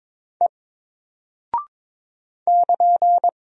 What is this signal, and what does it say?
Morse code, heard as a keyed tone, about 21 wpm, 700 Hz. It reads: E <KN>
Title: Single Character (Letter:Number:Punctuation:Prosign) Alt 22wpm